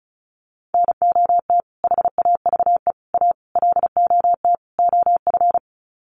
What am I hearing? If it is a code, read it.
Morse code, heard as a keyed tone, about 34 wpm, 700 Hz. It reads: NOT HAVE A LOT OF
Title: Sets of 5 Words 35wpm